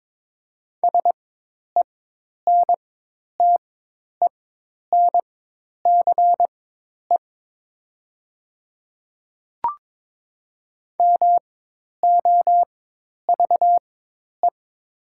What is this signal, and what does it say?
Morse code, heard as a keyed tone, about 22 wpm, spaced as for 10 wpm, 700 Hz. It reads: SENTENCE MOVE
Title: 4x Character Spacing - Top 300 Words - 22wpm